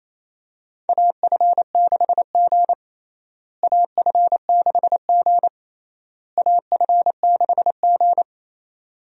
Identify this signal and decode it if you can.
Morse code, heard as a keyed tone, about 28 wpm, 700 Hz. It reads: AF6G AF6G AF6G
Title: USA Call Signs - 4 Characters - Repeated 3x - 28wpm